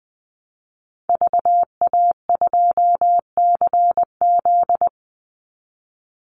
Morse code, heard as a keyed tone, about 20 wpm, 700 Hz. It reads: VA2CZ
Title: Call Signs Repeated 3x - 20wpm